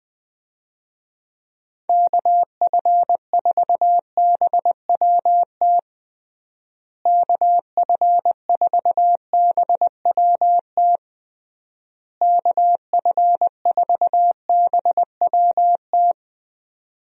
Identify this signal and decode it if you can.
Morse code, heard as a keyed tone, about 20 wpm, 700 Hz. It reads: KF4BWT KF4BWT KF4BWT
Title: USA Call Signs - 6 Characters - Repeated 3x - 20wpm